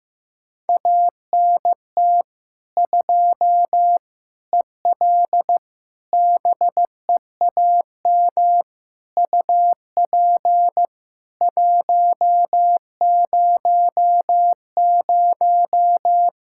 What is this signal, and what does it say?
Morse code, heard as a keyed tone, about 15 wpm, 700 Hz. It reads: ANT 2 EL BEAM UP 100
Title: K1USN SST QSO Practice 15wpm